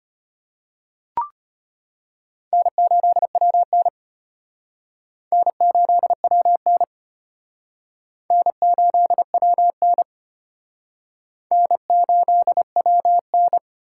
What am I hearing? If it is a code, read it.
Morse code, heard as a keyed tone, about 32 wpm, 700 Hz. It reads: N8WN N8WN N8WN N8WN